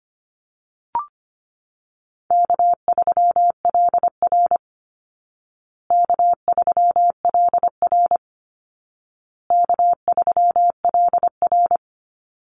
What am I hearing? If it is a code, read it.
Morse code, heard as a keyed tone, about 25 wpm, 700 Hz. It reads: K3LR K3LR K3LR
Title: Call Signs Repeated 2x - 25wpm